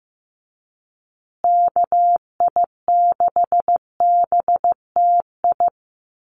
Morse code, heard as a keyed tone, about 15 wpm, 700 Hz. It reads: KI6BTI
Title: USA Call Signs Repeated 2x 15wpm